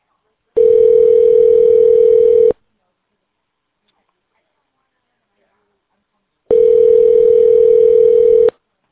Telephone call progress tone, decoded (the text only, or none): ringback tone